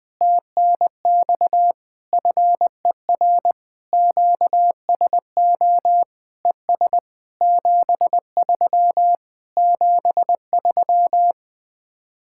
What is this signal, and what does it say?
Morse code, heard as a keyed tone, about 20 wpm, 700 Hz. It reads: TNX FER QSO ES 73 73
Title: Speed-Racing Common QSO Phrases Encore - 17wpm